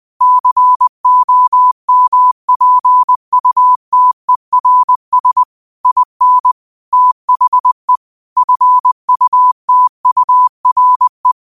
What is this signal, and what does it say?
Morse code, heard as a keyed tone, about 20 wpm, 1000 Hz. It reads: COMPUTERS IN THE FUTURE